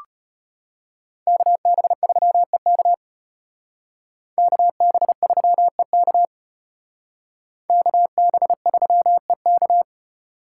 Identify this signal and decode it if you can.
Morse code, heard as a keyed tone, about 32 wpm, 700 Hz. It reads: KB3EK KB3EK KB3EK